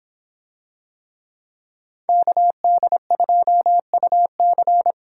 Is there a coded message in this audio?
Yes: Morse code from 2.1 to 4.9 s, heard as a keyed tone, about 25 wpm, 700 Hz. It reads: KD2UC